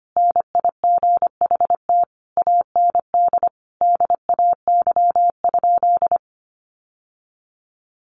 Morse code, heard as a keyed tone, about 25 wpm, 700 Hz. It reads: NIGHT AND DAY?
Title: Sentences from Top 1100 Words 25wpm